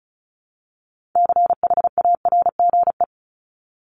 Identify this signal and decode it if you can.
Morse code, heard as a keyed tone, about 34 wpm, 700 Hz. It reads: CHARGE
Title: Top 751-1000 Words 35wpm